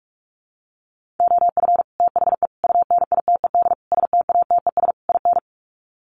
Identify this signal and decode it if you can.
Morse code, heard as a keyed tone, about 44 wpm, 700 Hz. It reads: OF THE UNITED STATES IN
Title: Sets of 5 Words Encore 45wpm